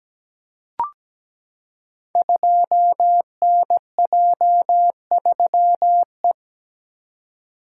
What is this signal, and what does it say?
Morse code, heard as a keyed tone, about 17 wpm, 700 Hz. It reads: 2NJ3E
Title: Code Groups - 5 Letter:Numbers - 17wpm